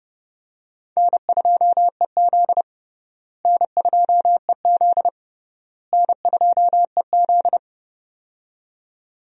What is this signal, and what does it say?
Morse code, heard as a keyed tone, about 32 wpm, 700 Hz. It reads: N2EZ N2EZ N2EZ
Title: USA Call Signs - 4 Characters - Repeated 3x - 30wpm